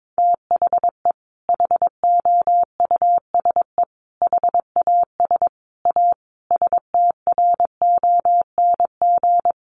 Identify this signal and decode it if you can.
Morse code, heard as a keyed tone, about 22 wpm, 700 Hz. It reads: THE HOUSE HAS A STRONG
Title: Sentences from Top 1200 Words 22wpm